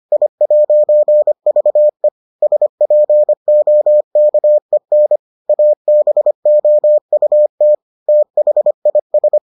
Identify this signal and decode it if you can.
Morse code, heard as a keyed tone, about 25 wpm, 600 Hz. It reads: I'VE SPOKEN ABOUT THIS